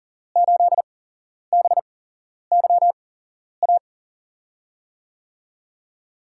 Morse code, heard as a keyed tone, about 39 wpm, spaced as for 11 wpm, 700 Hz. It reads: 8BYA
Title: USA Call Signs - 8x Character Spacing - 40wpm